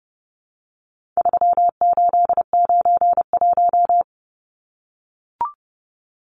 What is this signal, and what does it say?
Morse code, heard as a keyed tone, about 30 wpm, 700 Hz. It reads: 3891
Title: Numbers - 4-Digit 30wpm